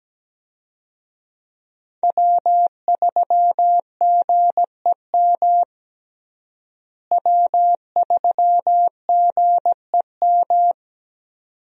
Morse code, heard as a keyed tone, about 17 wpm, 700 Hz. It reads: W3GEM W3GEM
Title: USA Call Signs - 5 Characters - Repeated 2x - 17wpm